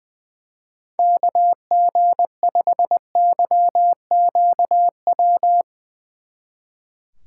Morse code, heard as a keyed tone, about 22 wpm, 700 Hz. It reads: KG5YQW
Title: Speed Racing - USA 2x3 Call Signs 20wpm